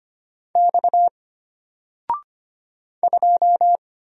Morse code, heard as a keyed tone, about 25 wpm, 700 Hz. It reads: X 2
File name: Single-Letter-Number-25wpm